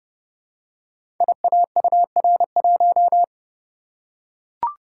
Morse code, heard as a keyed tone, about 30 wpm, 700 Hz. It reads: IAUR1
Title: Code Groups - 5 Letter:Numbers - 30wpm